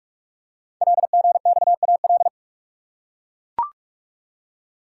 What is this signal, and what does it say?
Morse code, heard as a keyed tone, about 44 wpm, 700 Hz. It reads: LGXAL
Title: Code Groups - 5 Letters - 45wpm